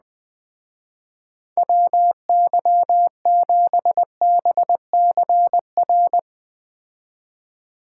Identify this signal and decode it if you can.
Morse code, heard as a keyed tone, about 20 wpm, 700 Hz. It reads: WY7BCR
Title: Speed Racing - USA 2x3 Call Signs 17wpm